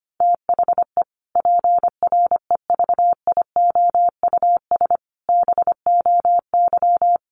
Morse code, heard as a keyed tone, about 25 wpm, 700 Hz. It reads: THE PREVIOUS BOY
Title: Sentences from Top 1500 Words 25wpm